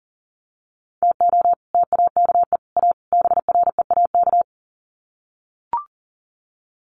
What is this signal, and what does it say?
Morse code, heard as a keyed tone, about 40 wpm, 700 Hz. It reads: TO TAKE A BREAK
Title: Sets of 4 Words 40wpm